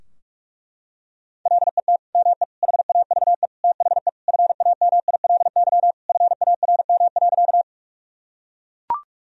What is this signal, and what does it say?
Morse code, heard as a keyed tone, about 41 wpm, 700 Hz. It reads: LET ME HAVE THE FAMILY FARM.
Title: Sentences from Top 1000 Words - Review Part 2 - 45wpm